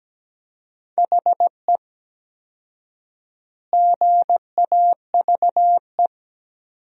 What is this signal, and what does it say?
Morse code, heard as a keyed tone, about 17 wpm, 700 Hz. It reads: HE GAVE
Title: 4x Word-Spacing - Sentences from Top 500 Words - 17wpm